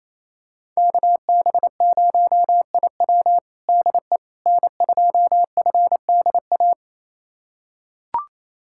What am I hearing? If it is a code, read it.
Morse code, heard as a keyed tone, about 28 wpm, 700 Hz. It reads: KB0IW DE N2FDA
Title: USA Callsign DE Callsign 28wpm